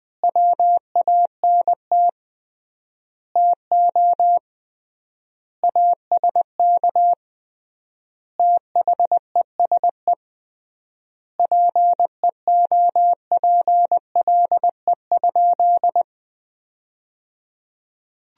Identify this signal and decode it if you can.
Morse code, heard as a keyed tone, about 20 wpm, 700 Hz. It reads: WANT TO ASK THESE PEOPLE?
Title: 3x Word-Spacing - Sentences from Top 200 Words - 20wpm